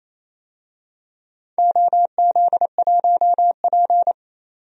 Morse code, heard as a keyed tone, about 28 wpm, 700 Hz. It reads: OZ1P
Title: DXCC-100 Call Signs - 4 Characters - 28wpm